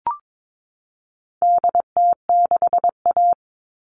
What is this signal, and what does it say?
Morse code, heard as a keyed tone, about 22 wpm, 700 Hz. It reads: DT6A
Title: All Cut Numbers - 4-Digit 22wpm